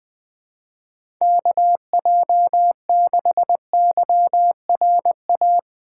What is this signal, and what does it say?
Morse code, heard as a keyed tone, about 20 wpm, 700 Hz. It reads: KJ6YRA
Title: USA Call Signs - 6 Characters - 4x Character Spacing - 20wpm